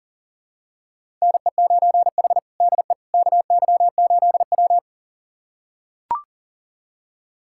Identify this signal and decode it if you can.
Morse code, heard as a keyed tone, about 40 wpm, 700 Hz. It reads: NE9H DE KY8W
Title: USA Callsign DE Callsign - 4 Characters - 40wpm